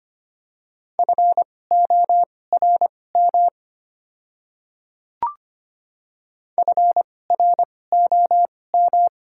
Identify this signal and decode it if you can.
Morse code, heard as a keyed tone, about 25 wpm, spaced as for 18 wpm, 700 Hz. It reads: FORM FROM
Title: 2x Character Spacing - Top 200 Words - 25wpm